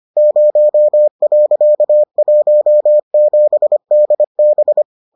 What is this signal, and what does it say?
Morse code, heard as a keyed tone, about 25 wpm, 600 Hz. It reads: 0.17DB